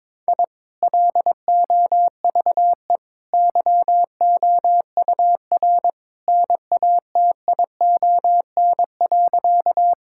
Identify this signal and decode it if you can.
Morse code, heard as a keyed tone, about 22 wpm, 700 Hz. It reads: I LOVE YOUR NATION.
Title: Sentences from Top 1000 Words - Review Part 6 - 22wpm